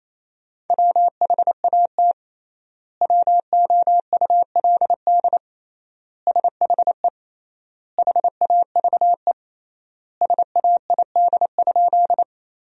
Morse code, heard as a keyed tone, about 28 wpm, 700 Hz. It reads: WHAT WOULD SHE HAVE SAID?
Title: 3x Word-Spacing - Sentences from Top 100 Words - 28wpm